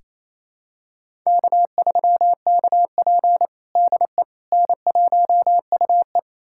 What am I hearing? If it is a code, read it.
Morse code, heard as a keyed tone, about 28 wpm, 700 Hz. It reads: K3KP DE N1UE
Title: USA Callsign DE Callsign - 4 Characters - 28wpm